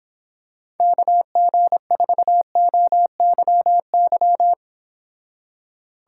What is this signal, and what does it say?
Morse code, heard as a keyed tone, about 25 wpm, 700 Hz. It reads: KG4OYY